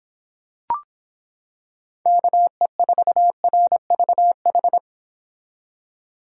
Morse code, heard as a keyed tone, about 26 wpm, 700 Hz. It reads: KE4RVH